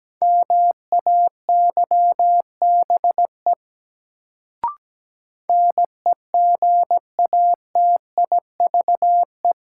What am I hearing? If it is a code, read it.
Morse code, heard as a keyed tone, about 17 wpm, 700 Hz. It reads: MAYBE NEGATIVE
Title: Sentences from Top 1000 Words - New Word Drill - 17wpm